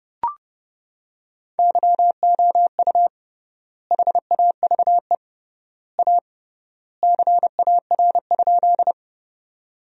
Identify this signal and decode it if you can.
Morse code, heard as a keyed tone, about 29 wpm, 700 Hz. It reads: YOU HAVE A CAR?
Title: 3x Word-Spacing - Sentences from Top 300 Words - 30wpm